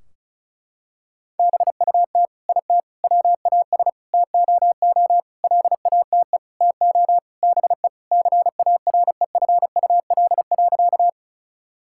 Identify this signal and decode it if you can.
Morse code, heard as a keyed tone, about 36 wpm, 700 Hz. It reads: BUT IT WAS TOO LATE TO BE CAREFUL.
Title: Sentences from Top 1600 Words 35wpm